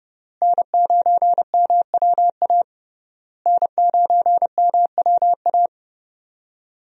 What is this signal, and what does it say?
Morse code, heard as a keyed tone, about 31 wpm, 700 Hz. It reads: N9MWA N9MWA
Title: USA Call Signs - 5 Characters - Repeated 3x - 30wpm